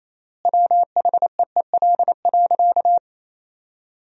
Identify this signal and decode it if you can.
Morse code, heard as a keyed tone, about 28 wpm, 700 Hz. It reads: WHEEL.
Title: 3x Word-Spacing - Sentences from Top 500 Words - 28wpm